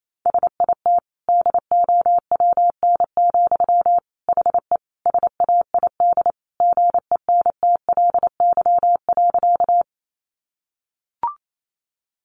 Morse code, heard as a keyed tone, about 28 wpm, 700 Hz. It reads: SIT DOWN, HE SAID GENTLY.
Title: Sentences from Top 1500 Words 28wpm